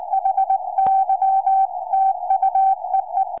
Morse code, heard as a keyed tone, about 19 wpm, 770 Hz. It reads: HY TU EE
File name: tropo / ua2fl 2m